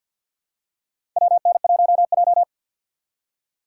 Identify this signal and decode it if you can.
Morse code, heard as a keyed tone, about 52 wpm, 700 Hz. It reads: WN1J
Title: USA Call Signs - 4 Characters - 2x Character Spacing - 50wpm